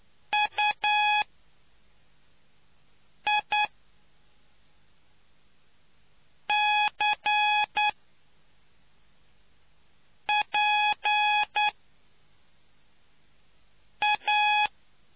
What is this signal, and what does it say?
Morse code, heard as a keyed tone, about 9 wpm, spaced as for 3 wpm, 840 Hz. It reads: UICPA